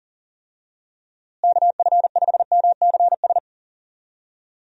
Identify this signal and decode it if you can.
Morse code, heard as a keyed tone, about 37 wpm, 700 Hz. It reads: KF5MCS
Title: USA Call Signs - 6 Characters - Repeated 2x - 40wpm